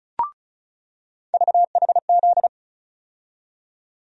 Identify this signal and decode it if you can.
Morse code, heard as a keyed tone, about 36 wpm, 700 Hz. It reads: VHZ
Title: USA Call Sign Postfixes - 3 Letters 35wpm